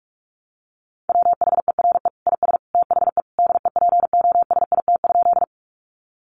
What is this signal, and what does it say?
Morse code, heard as a keyed tone, about 44 wpm, 700 Hz. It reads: WHERE IS THE DEPOSIT?
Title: Sentences from Top 1300 Words 45wpm